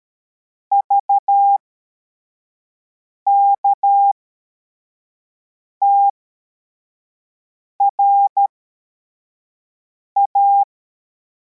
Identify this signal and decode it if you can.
Morse code, heard as a keyed tone, about 13 wpm, spaced as for 4 wpm, 800 Hz. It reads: VKTRA
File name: formation / cw / Text52